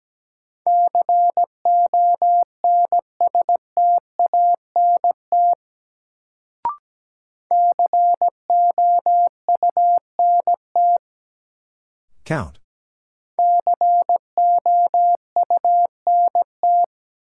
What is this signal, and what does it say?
Morse code, heard as a keyed tone, about 17 wpm, 700 Hz. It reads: CONSTANT COUNT COUNT
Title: Sentences from Top 1500 Words - New Word Drill - 17wpm